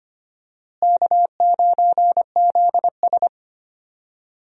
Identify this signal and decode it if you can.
Morse code, heard as a keyed tone, about 25 wpm, 700 Hz. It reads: K9ZS
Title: USA Call Signs - 4 Characters - Repeated 3x - 25wpm